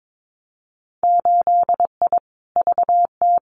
Morse code, heard as a keyed tone, about 22 wpm, 700 Hz. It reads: 8I VT
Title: Winter Field Day Exchanges Encore 22wpm